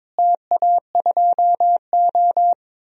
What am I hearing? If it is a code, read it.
Morse code, heard as a keyed tone, about 22 wpm, 700 Hz. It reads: TA2O
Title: DXCC-100 Call Signs - 4 Characters - Repeated 2x - 22wpm